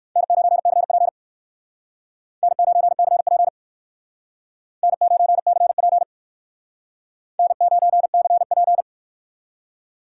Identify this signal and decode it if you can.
Morse code, heard as a keyed tone, about 62 wpm, 700 Hz. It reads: N9CP N9CP N9CP N9CP